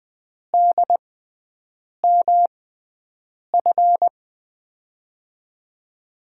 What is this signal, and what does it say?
Morse code, heard as a keyed tone, about 20 wpm, spaced as for 7 wpm, 700 Hz. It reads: DMF
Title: USA Call Signs - 6x Character Spacing - 20wpm